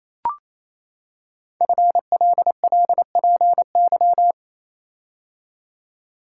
Morse code, heard as a keyed tone, about 28 wpm, 700 Hz. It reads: FLLPY
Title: Code Groups - 5 Letters - 28wpm